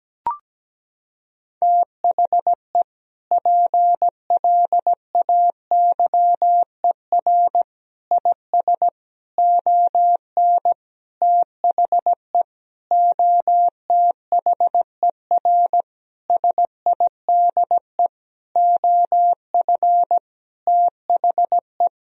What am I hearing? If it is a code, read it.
Morse code, heard as a keyed tone, about 17 wpm, 700 Hz. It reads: THE PLAYER IS ON THE OTHER SIDE OF THE
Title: Sentences from Top 1000 Words - Review Part 4 - 17wpm